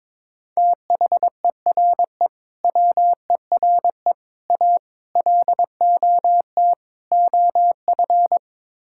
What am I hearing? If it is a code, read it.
Morse code, heard as a keyed tone, about 22 wpm, 700 Hz. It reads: THERE WERE A LOT OF
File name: practice / 22wpm / Build A Phrase - 5-Word Phrases - 22wpm